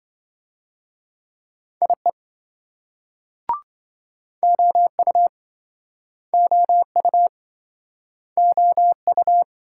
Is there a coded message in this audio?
Yes: Morse code from 1.8 to 9.4 s, heard as a keyed tone, about 27 wpm, 700 Hz. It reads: IE OU OU OU